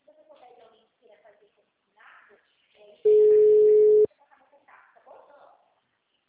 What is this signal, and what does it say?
Ringback tone (Europe, 425 Hz), with speech.
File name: out- -2033- - - .8596